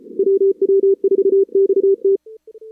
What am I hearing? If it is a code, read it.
Morse code, heard as a keyed tone, about 32 wpm, 440 Hz. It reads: WW4XT
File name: CWSprint / Feb2022 / k5tr-feb-na-sprint-cw-k5ot -0000